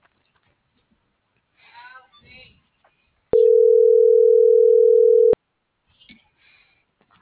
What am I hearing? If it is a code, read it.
Ringback tone (North America), 440 + 480 Hz.